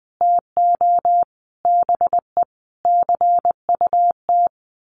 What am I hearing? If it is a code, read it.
Morse code, heard as a keyed tone, about 20 wpm, 700 Hz. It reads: TO BE CUT
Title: A Sound Of Thunder 20wpm